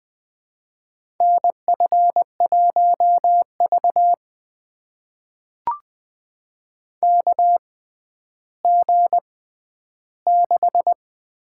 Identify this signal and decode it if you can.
Morse code, heard as a keyed tone, about 20 wpm, spaced as for 11 wpm, 700 Hz. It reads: NF1V KG6
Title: USA Call Signs - 6x Character Spacing - 20wpm